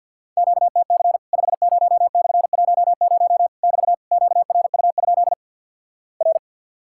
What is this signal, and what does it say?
Morse code, heard as a keyed tone, about 50 wpm, 700 Hz. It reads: QTC 50/10 <BT> QRU? R
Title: Worked All Europe DX Contest - QTC Exchange with Repeats - 50wpm